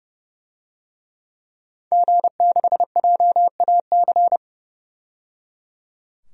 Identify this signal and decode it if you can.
Morse code, heard as a keyed tone, about 30 wpm, 700 Hz. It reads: G6JAC
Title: DXCC-100 Call Signs - 5 Characters - 30wpm